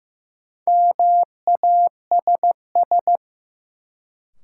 Morse code, heard as a keyed tone, about 15 wpm, 700 Hz. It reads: MASS
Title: Sets of 5 Words Encore - New Word Drill - 15wpm